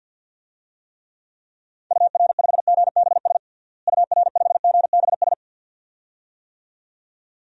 Morse code, heard as a keyed tone, about 50 wpm, 700 Hz. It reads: UR5GBS UR5GBS
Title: DXCC-100 Call Signs - 6 Characters - Repeated 2x - 50wpm